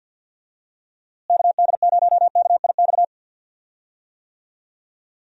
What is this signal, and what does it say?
Morse code, heard as a keyed tone, about 51 wpm, 700 Hz. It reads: KD0KIX
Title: USA Call Signs - 6 Characters - Repeated 3x - 50wpm